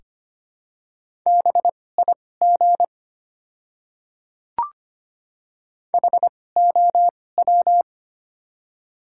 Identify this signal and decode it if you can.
Morse code, heard as a keyed tone, about 25 wpm, spaced as for 18 wpm, 700 Hz. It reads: BIG HOW
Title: 2x Character Spacing - Top 200 Words - 25wpm